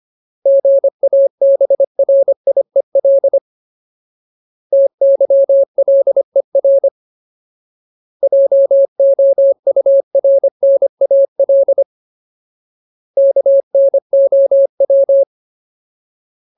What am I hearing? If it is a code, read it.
Morse code, heard as a keyed tone, about 25 wpm, 550 Hz. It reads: GABRIEL TYLER JOURNAL KNOW
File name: Ultimate Word List / 25WPM Ultimate